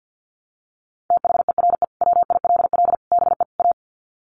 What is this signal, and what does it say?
Morse code, heard as a keyed tone, about 49 wpm, 700 Hz. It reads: THERE WILL BE A
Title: Sets of 4 Words 50wpm